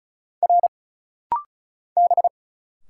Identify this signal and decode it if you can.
Morse code, heard as a keyed tone, about 36 wpm, 700 Hz. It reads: R B
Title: Single Letter-Number Alt - Rapid-Fire 35wpm